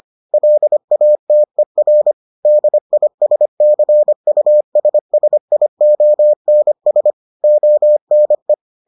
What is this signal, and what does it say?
Morse code, heard as a keyed tone, about 25 wpm, 600 Hz. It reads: LATER DISCUSSIONS ONE